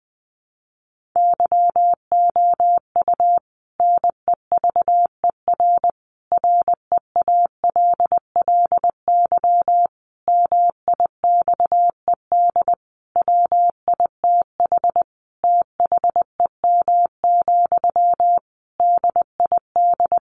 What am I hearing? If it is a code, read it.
Morse code, heard as a keyed tone, about 20 wpm, 700 Hz. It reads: YOU NEVER REALLY MIXED WITH THEM, DID
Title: Sentences from Top 1500 Words 20wpm